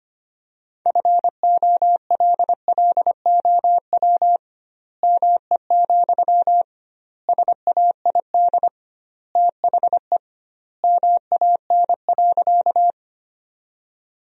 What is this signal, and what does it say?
Morse code, heard as a keyed tone, about 25 wpm, 700 Hz. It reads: FOLLOW ME, SAID THE MAN.
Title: 2x Word-Spacing - Sentences from Top 200 Words - 25wpm